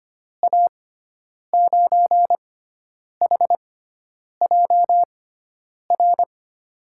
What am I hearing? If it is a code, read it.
Morse code, heard as a keyed tone, about 25 wpm, spaced as for 9 wpm, 700 Hz. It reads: A9HJR
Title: USA Call Signs - 6 Characters - 6x Character Spacing - 25wpm